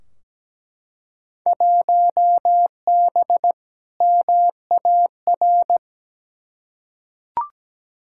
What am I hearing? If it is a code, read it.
Morse code, heard as a keyed tone, about 17 wpm, 700 Hz. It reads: 1B MAR
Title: Field Day Exchanges Encore - 17wpm